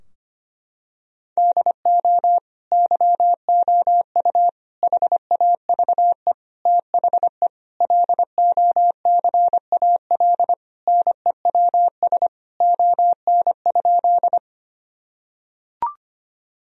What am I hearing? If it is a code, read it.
Morse code, heard as a keyed tone, about 25 wpm, 700 Hz. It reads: DO YOU HAVE THE LOCAL NEWS ON?
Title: Sentences from Top 1000 Words - Review Part 3 - 25wpm